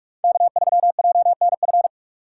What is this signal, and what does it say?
Morse code, heard as a keyed tone, about 41 wpm, 700 Hz. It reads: K3JNF
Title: USA Call Signs - 5 Characters - 4x Character Spacing - 45wpm